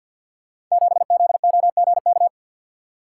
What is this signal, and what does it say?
Morse code, heard as a keyed tone, about 51 wpm, 700 Hz. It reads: 7ZOGK
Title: Code Groups - 5 Letter:Numbers - 50wpm